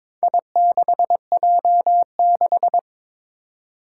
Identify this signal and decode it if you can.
Morse code, heard as a keyed tone, about 22 wpm, 700 Hz. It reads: I6J6
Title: Four Challenging Letter-Number Combinations 22wpm